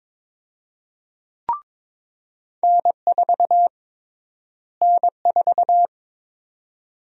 Morse code, heard as a keyed tone, about 22 wpm, 700 Hz. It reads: N4 N4
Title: USA Call Sign Prefixes - Repeated 2x - 22wpm